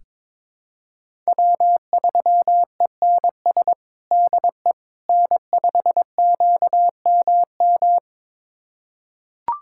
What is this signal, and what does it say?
Morse code, heard as a keyed tone, about 22 wpm, 700 Hz. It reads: W3ENS DE N5QMM
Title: USA Callsign DE Callsign - 5 Characters - 22wpm